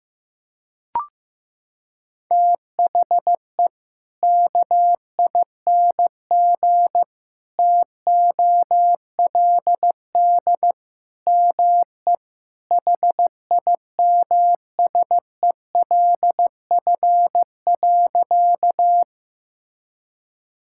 Morse code, heard as a keyed tone, about 15 wpm, 700 Hz. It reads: THE KING TOLD ME HIMSELF.
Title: Sentences from Top 1600 Words 15wpm